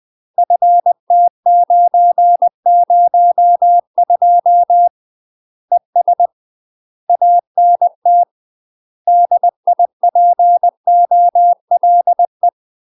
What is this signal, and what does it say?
Morse code, heard as a keyed tone, about 20 wpm, 700 Hz. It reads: FT902 ES ANT DIPOLE
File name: UK MASTER QSOs TRAINING / 20WPM / UK MASTER QSOs 0002